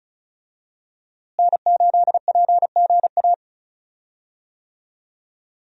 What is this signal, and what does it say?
Morse code, heard as a keyed tone, about 36 wpm, 700 Hz. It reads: N8PGA
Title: USA Call Signs - 5 Characters - Repeated 2x - 35wpm